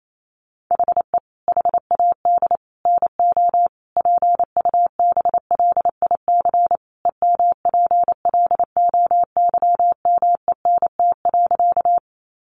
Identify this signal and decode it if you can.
Morse code, heard as a keyed tone, about 28 wpm, 700 Hz. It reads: HE HAD NO PUBLIC EMPLOYMENT.
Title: Sentences from Top 1500 Words 28wpm